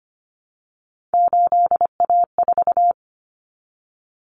Morse code, heard as a keyed tone, about 25 wpm, 700 Hz. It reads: 8A4
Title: Common Cut Numbers - 3-Digit 25wpm